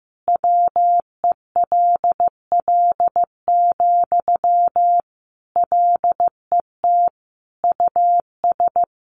Morse code, heard as a keyed tone, about 15 wpm, 700 Hz. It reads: WELL, LET US
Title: Sentences from Top 1200 Words 15wpm